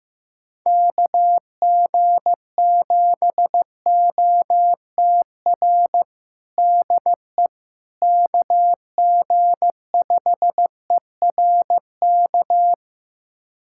Morse code, heard as a keyed tone, about 15 wpm, 700 Hz. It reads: KG7OTR DE KG5ERK
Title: USA Callsign DE Callsign - 6 Characters - 15wpm